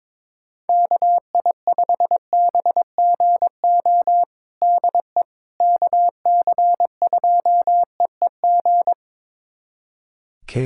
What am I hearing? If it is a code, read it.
Morse code, heard as a keyed tone, about 21 wpm, 700 Hz. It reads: KI5BGO DE KC2EEG
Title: USA Callsign DE Callsign 22wpm